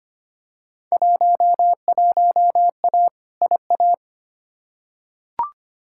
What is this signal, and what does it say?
Morse code, heard as a keyed tone, about 25 wpm, 700 Hz. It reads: 11A IA
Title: Field Day Exchanges Encore - 25wpm